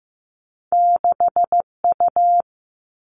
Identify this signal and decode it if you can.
Morse code, heard as a keyed tone, about 15 wpm, 700 Hz. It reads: 6U
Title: Two Challenging Letter-Number Combinations 15wpm